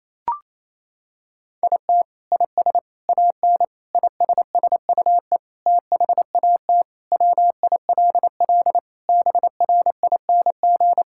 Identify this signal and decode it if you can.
Morse code, heard as a keyed tone, about 28 wpm, 700 Hz. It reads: IT IS AN ISSUE THAT WILL BRING
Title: Sentences from Top 1000 Words - Review Part 6 - 28wpm